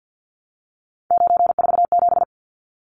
Morse code, heard as a keyed tone, about 50 wpm, 700 Hz. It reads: 947
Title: Numbers - 3-Digit 50wpm